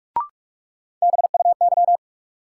Morse code, heard as a keyed tone, about 48 wpm, 700 Hz. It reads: BUY
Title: Sentences from Top 700 Words - New Word Drill - 45wpm